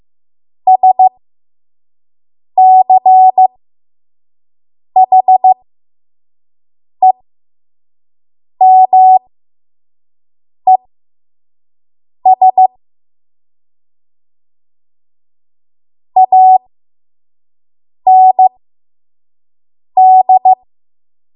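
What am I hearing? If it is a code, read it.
Morse code, heard as a keyed tone, about 15 wpm, spaced as for 5 wpm, 750 Hz. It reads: SCHEMES AND
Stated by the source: 5 WPM Morse code practice.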